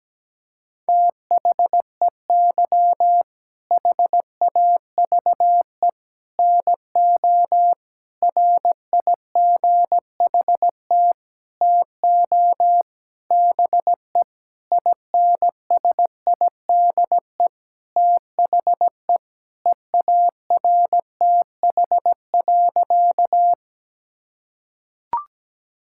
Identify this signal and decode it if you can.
Morse code, heard as a keyed tone, about 17 wpm, 700 Hz. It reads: THEY HAVE NO RIGHT TO BE INSIDE THE EARTH.
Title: Sentences from Top 1000 Words - Review Part 6 - 17wpm